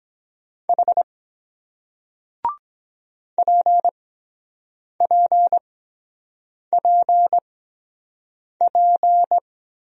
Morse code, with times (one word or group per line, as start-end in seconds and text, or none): 0.7-1.0 H
3.4-9.4 PPPP